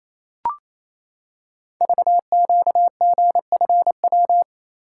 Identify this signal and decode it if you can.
Morse code, heard as a keyed tone, about 27 wpm, 700 Hz. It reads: VQGFW
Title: Code Groups - 5 Letters - 28wpm